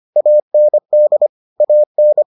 Morse code, heard as a keyed tone, about 25 wpm, 600 Hz. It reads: AND AN